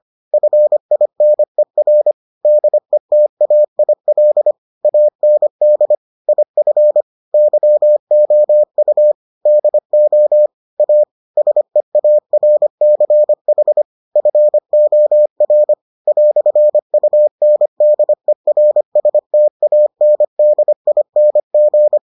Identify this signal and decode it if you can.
Morse code, heard as a keyed tone, about 25 wpm, 600 Hz. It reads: FINER DETAIL AND IF YOU DO A SEARCH FOR "UNDERSTANDING